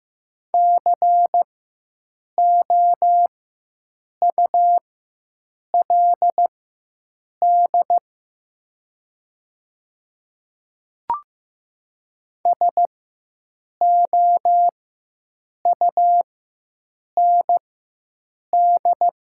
Morse code, heard as a keyed tone, about 15 wpm, spaced as for 7 wpm, 700 Hz. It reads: COULD SOUND
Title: 4x Character Spacing - Top 100 Words - 15wpm